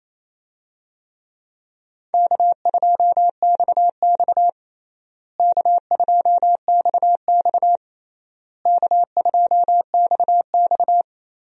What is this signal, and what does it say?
Morse code, heard as a keyed tone, about 28 wpm, 700 Hz. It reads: K2XX K2XX K2XX
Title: USA Call Signs - 4 Characters - Repeated 3x - 28wpm